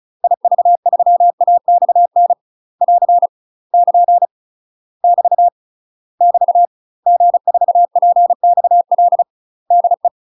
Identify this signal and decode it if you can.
Morse code, heard as a keyed tone, about 34 wpm, 700 Hz. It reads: IV3AXN <AR> <KN> <BT> <BT> G4PXL DE
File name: UK MASTER QSOs TRAINING / 35WPM / UK MASTER QSOs 0021